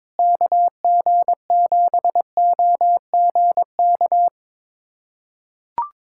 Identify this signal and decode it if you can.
Morse code, heard as a keyed tone, about 22 wpm, 700 Hz. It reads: KG7OGK
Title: USA Callsign DE Callsign 22wpm